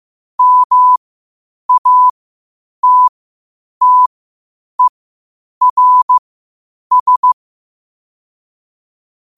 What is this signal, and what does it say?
Morse code, heard as a keyed tone, about 15 wpm, spaced as for 8 wpm, 1000 Hz. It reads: MATTERS
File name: 8wpm